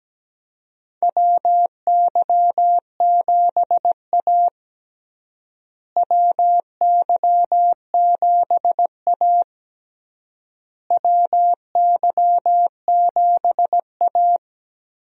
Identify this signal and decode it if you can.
Morse code, heard as a keyed tone, about 17 wpm, 700 Hz. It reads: WY7A WY7A WY7A
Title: USA Call Signs - 4 Characters - Repeated 3x - 17wpm